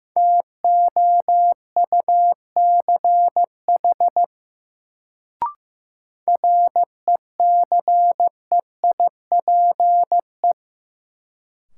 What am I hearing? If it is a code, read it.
Morse code, heard as a keyed tone, about 15 wpm, 700 Hz. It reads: TOUCH RECEIPE
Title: Sentences from Top 1200 Words - New Word Drill Alt - 15wpm